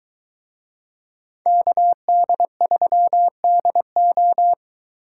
Morse code, heard as a keyed tone, about 23 wpm, 700 Hz. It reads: KD3DO